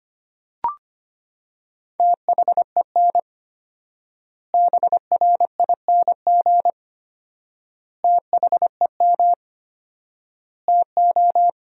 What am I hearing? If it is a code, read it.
Morse code, heard as a keyed tone, about 25 wpm, 700 Hz. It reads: THEN BRING THEM TO
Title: 4x Word-Spacing - Sentences from Top 500 Words - 25wpm